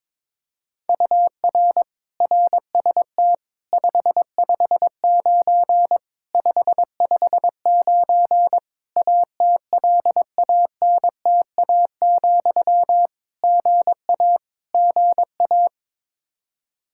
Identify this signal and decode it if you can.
Morse code, heard as a keyed tone, about 22 wpm, 700 Hz. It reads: UR RST 559 559 ATLANTA, GA GA
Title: K1USN SST QSO Practice 22wpm